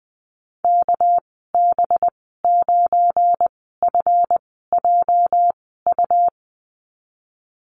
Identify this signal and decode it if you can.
Morse code, heard as a keyed tone, about 20 wpm, spaced as for 15 wpm, 700 Hz. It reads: KB9FJU
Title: 238 Lesson 38g - Letter q - Call Signs 20wpm